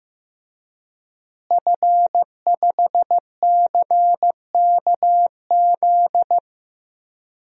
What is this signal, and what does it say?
Morse code, heard as a keyed tone, about 15 wpm, 700 Hz. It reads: F5CKZ
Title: DXCC-100 Call Signs - 5 Characters - Repeated 2x - 15wpm